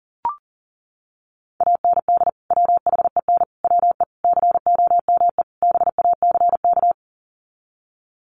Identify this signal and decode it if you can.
Morse code, heard as a keyed tone, about 40 wpm, 700 Hz. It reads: AND WHEN WE COME BACK
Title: Sets of 5 Words 40wpm